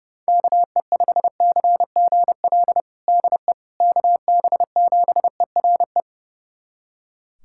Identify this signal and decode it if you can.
Morse code, heard as a keyed tone, about 32 wpm, 700 Hz. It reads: KE5CGL DE KB7ERE
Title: USA Callsign DE Callsign - 6 Characters - 30wpm